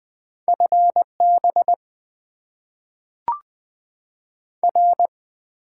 Morse code, heard as a keyed tone, about 20 wpm, 700 Hz. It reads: FB R
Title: Top 20 QSO Elements - 8x Character Spacing - 20wpm